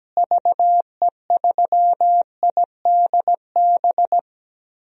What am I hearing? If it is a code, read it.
Morse code, heard as a keyed tone, about 17 wpm, 700 Hz. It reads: VE3IDB
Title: DXCC-100 Call Signs - 6 Characters - Repeated 2x - 17wpm